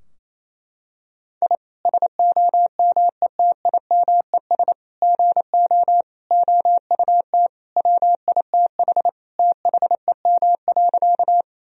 Morse code, heard as a keyed tone, about 28 wpm, 700 Hz. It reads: I SOMETIMES GO OUT WITH THEM.
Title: Sentences from Top 1000 Words - Review Part 3 - 28wpm